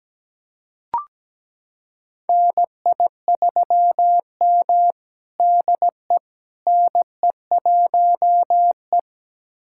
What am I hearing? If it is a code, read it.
Morse code, heard as a keyed tone, about 17 wpm, 700 Hz. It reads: NI3M DE NE1E
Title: USA Callsign DE Callsign - 4 Characters - 17wpm